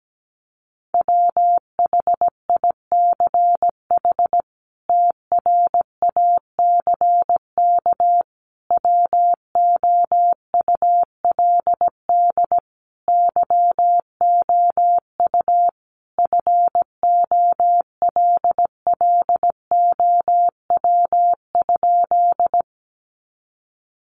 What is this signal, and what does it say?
Morse code, heard as a keyed tone, about 17 wpm, 700 Hz. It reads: WHICH TRACK WOULD YOU FOLLOW?
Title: Sentences from Top 1100 Words 17wpm